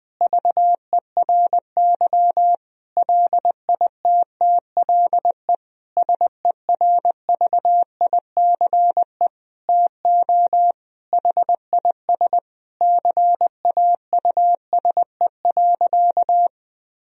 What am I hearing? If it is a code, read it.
Morse code, heard as a keyed tone, about 20 wpm, 700 Hz. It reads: VERY LITTLE SERVICE TO HIS CAUSE.
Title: Sentences from Top 1000 Words - Review Part 4 - 20wpm